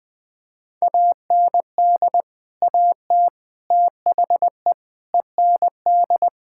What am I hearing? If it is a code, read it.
Morse code, heard as a keyed tone, about 20 wpm, 700 Hz. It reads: AND AT THE END
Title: Build A Phrase - 5-Word Phrases - 20wpm